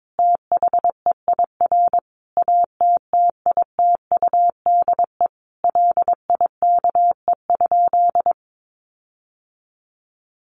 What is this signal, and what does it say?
Morse code, heard as a keyed tone, about 22 wpm, 700 Hz. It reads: THEIR ATTITUDE LIKE?
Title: Sentences from Top 1300 Words 22wpm